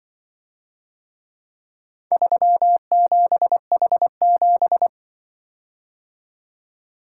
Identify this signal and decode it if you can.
Morse code, heard as a keyed tone, about 24 wpm, 700 Hz. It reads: 37H7
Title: Speed-Racing - Four Letter-Number Combinations - 20wpm